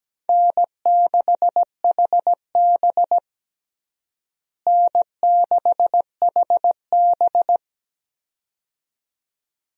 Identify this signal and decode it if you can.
Morse code, heard as a keyed tone, about 17 wpm, 700 Hz. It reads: N6HB N6HB
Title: USA Call Signs - 4 Characters - Repeated 2x - 17wpm